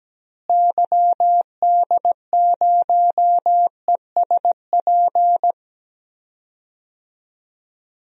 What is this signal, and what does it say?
Morse code, heard as a keyed tone, about 17 wpm, 700 Hz. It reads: YD0ESP
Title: DXCC-100 Call Signs - 6 Characters - Repeated 2x - 17wpm